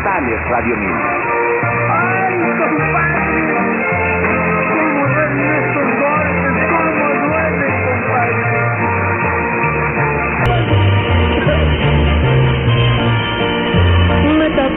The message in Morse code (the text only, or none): none